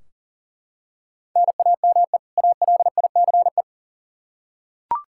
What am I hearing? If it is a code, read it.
Morse code, heard as a keyed tone, about 39 wpm, 700 Hz. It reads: NAME ALICE
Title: Practice QSO - Exchange 1 - Beginning - 40wpm